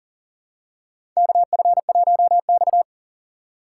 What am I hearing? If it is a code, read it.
Morse code, heard as a keyed tone, about 40 wpm, 700 Hz. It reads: KF1X
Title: USA Call Signs - 4 Characters - 6x Character Spacing - 40wpm